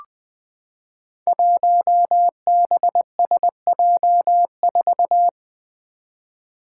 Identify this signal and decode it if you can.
Morse code, heard as a keyed tone, about 20 wpm, 700 Hz. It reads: 1BSJ4
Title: Five Challenging Letter-Number Combinations 20wpm